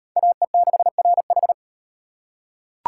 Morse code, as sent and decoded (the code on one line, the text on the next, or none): .- . -.... .-. ....
AE6RH